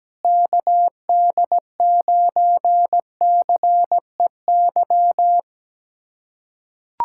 Morse code, heard as a keyed tone, about 17 wpm, 700 Hz. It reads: KD9CEY
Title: USA Callsign DE Callsign 17wpm